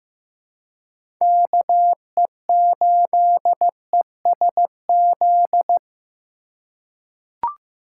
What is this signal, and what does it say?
Morse code, heard as a keyed tone, about 15 wpm, 700 Hz. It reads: KE8ESZ
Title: USA Call Signs - 6 Characters - 3x Character Spacing - 15wpm